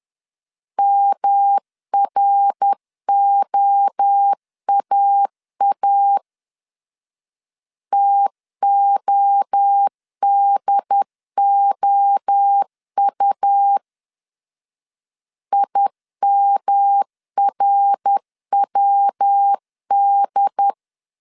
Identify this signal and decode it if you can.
Morse code, heard as a keyed tone, about 10 wpm, 800 Hz. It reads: MROAA TODOU IMRWD
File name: CW / DET / 1 3 1500C S C 10wpm 3335